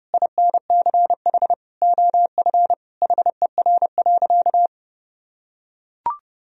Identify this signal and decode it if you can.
Morse code, heard as a keyed tone, about 30 wpm, 700 Hz. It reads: INCH OF HER.
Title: Sentences from Top 1000 Words - Review Part 3 - 30wpm